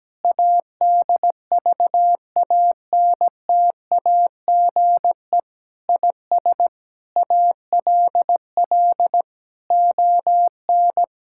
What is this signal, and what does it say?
Morse code, heard as a keyed tone, about 17 wpm, 700 Hz. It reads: ADVANTAGE IS ALL ON
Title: Sentences from Top 1000 Words - Review Part 6 - 17wpm